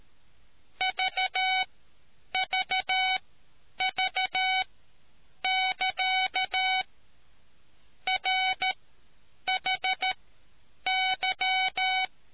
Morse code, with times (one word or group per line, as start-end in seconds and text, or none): 0.8-6.8 VVV<KA>
8.1-12.1 RHY